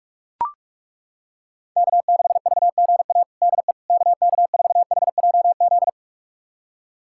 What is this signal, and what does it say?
Morse code, heard as a keyed tone, about 41 wpm, 700 Hz. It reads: K6VGA DE KK4HJZ
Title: USA Callsign DE Callsign 45wpm